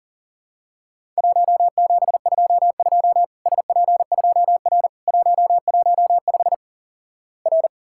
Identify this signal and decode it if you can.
Morse code, heard as a keyed tone, about 40 wpm, 700 Hz. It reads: 1722 SP2R 115 R
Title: Worked All Europe DX Contest - QTC Exchange - 40wpm